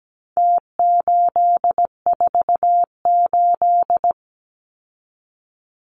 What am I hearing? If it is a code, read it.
Morse code, heard as a keyed tone, about 17 wpm, 700 Hz. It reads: T848
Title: Numbers - Contest Style - 4-Digit 17wpm